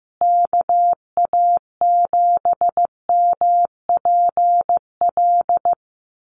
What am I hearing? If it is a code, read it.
Morse code, heard as a keyed tone, about 15 wpm, 700 Hz. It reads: KA7MPL
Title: USA Call Signs Repeated 2x 15wpm